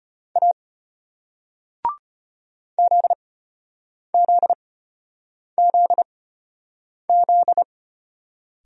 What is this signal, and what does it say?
Morse code, heard as a keyed tone, about 33 wpm, spaced as for 8 wpm, 700 Hz. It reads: A ZZZZ